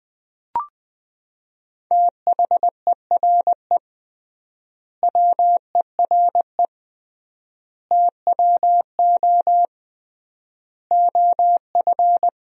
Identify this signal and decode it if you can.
Morse code, heard as a keyed tone, about 20 wpm, 700 Hz. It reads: THERE WERE TWO OF
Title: 3x Word-Spacing - Sentences from Top 100 Words - 20wpm